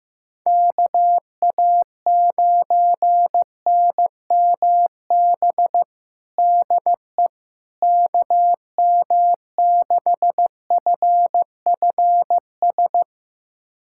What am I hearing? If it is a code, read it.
Morse code, heard as a keyed tone, about 15 wpm, 700 Hz. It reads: KA9NMB DE KM6FFS
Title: USA Callsign DE Callsign 15wpm